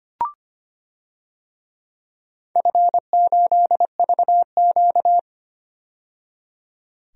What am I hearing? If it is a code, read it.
Morse code, heard as a keyed tone, about 25 wpm, 700 Hz. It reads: F8VQ
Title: DXCC-100 Call Signs - 4 Characters - 25wpm